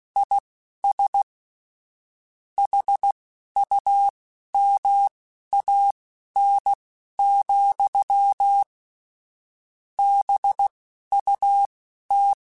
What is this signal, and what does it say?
Morse code, heard as a keyed tone, about 16 wpm, spaced as for 11 wpm, 800 Hz. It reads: IS HUMAN, BUT
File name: Chuckles / Chuckles-mp3s / HumanErr @10